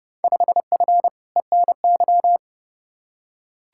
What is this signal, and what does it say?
Morse code, heard as a keyed tone, about 30 wpm, 700 Hz. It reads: 5F ENY
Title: Field Day Exchanges Encore - 30wpm